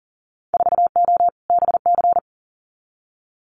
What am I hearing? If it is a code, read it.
Morse code, heard as a keyed tone, about 40 wpm, 700 Hz. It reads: 4O BC
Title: Winter Field Day Exchanges Encore 40wpm